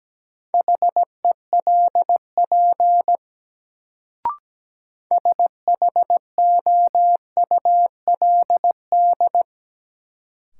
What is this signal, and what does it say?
Morse code, heard as a keyed tone, about 17 wpm, 700 Hz. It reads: HELP SHOULD
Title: Sets of 2 Words - New Word Drill - 17wpm